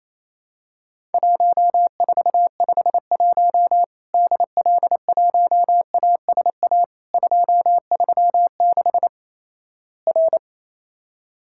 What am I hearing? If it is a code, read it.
Morse code, heard as a keyed tone, about 28 wpm, 700 Hz. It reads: 1451 DL1ASA 236 R
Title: Worked All Europe DX Contest - QTC Exchange - 28wpm